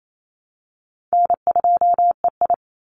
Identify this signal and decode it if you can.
Morse code, heard as a keyed tone, about 28 wpm, 700 Hz. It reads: N2EI
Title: Easy Call Signs Repeated 2x - 28wpm